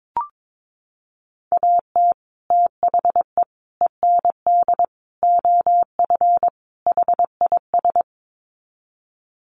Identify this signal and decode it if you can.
Morse code, heard as a keyed tone, about 22 wpm, 700 Hz. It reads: AT THE END OF HIS
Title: Sets of 5 Words Encore 22wpm